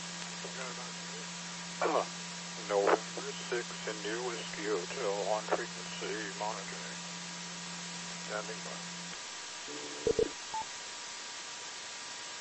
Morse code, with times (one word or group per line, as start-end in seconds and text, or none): none